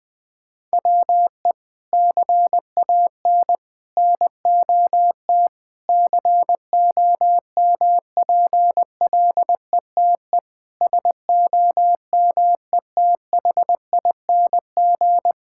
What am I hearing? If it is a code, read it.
Morse code, heard as a keyed tone, about 21 wpm, 700 Hz. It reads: WE CAN NOT COMPLETE SOMETHING
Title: Sentences from Top 1000 Words - Review Part 6 - 20wpm